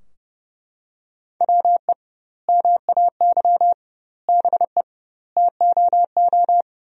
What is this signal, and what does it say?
Morse code, heard as a keyed tone, about 30 wpm, 700 Hz. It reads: WE MAY BE TOO
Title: 2x Word-Spacing - Sentences from Top 300 Words - 30wpm